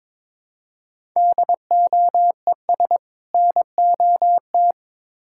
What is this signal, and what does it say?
Morse code, heard as a keyed tone, about 22 wpm, 700 Hz. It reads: DOES NOT
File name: Build A Phrase - 2-Word Phrases - 22wpm